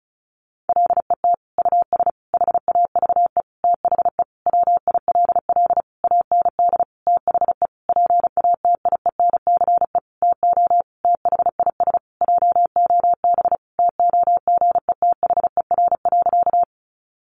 Morse code, heard as a keyed tone, about 34 wpm, 700 Hz. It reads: LET US HAVE THE WILL AND THE PATIENCE TO THIS JOB TOGETHER.
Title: Sentences from Top 1300 Words 35wpm